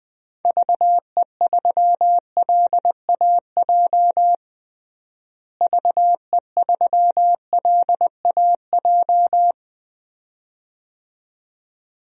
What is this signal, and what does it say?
Morse code, heard as a keyed tone, about 21 wpm, 700 Hz. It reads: VE3LAJ VE3LAJ
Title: DXCC-100 Call Signs - 6 Characters - Repeated 2x - 20wpm